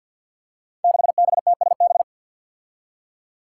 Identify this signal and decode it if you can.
Morse code, heard as a keyed tone, about 48 wpm, 700 Hz. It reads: 6BTSB
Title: Five Challenging Letter-Number Combinations 50wpm